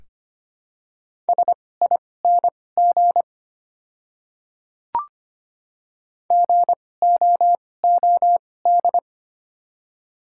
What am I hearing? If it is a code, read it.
Morse code, heard as a keyed tone, about 25 wpm, spaced as for 18 wpm, 700 Hz. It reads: SING GOOD
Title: 2x Character Spacing - Top 500 Words - 25wpm